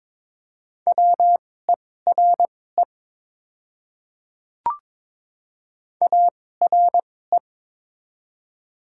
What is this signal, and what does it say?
Morse code, heard as a keyed tone, about 22 wpm, spaced as for 16 wpm, 690 Hz. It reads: WERE ARE
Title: 2x Character Spacing - Top 100 Words - 22wpm